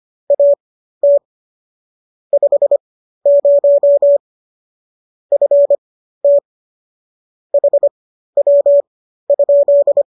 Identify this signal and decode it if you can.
Morse code, heard as a keyed tone, about 25 wpm, spaced as for 13 wpm, 570 Hz. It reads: AT 50 FT HW?